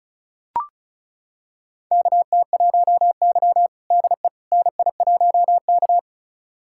Morse code, heard as a keyed tone, about 36 wpm, 700 Hz. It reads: KT1Y DE NI1K
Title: USA Callsign DE Callsign - 4 Characters - 35wpm